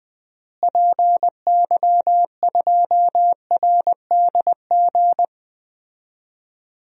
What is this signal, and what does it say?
Morse code, heard as a keyed tone, about 19 wpm, 700 Hz. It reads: PY2RDG
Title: DXCC-100 Call Signs - 6 Characters - Repeated 3x - 20wpm